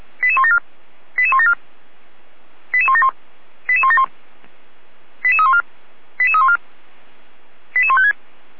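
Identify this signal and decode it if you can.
Five-tone selective call (ZVEI), heard as quick runs of five tones, tones of about 70 ms. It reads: 88174 88174 88171 88171 88314 88314 88156
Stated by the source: ZVEI five-tone call received by SDR, archived by feuerwehr-orlamuende.org (Orlamünde fire brigade).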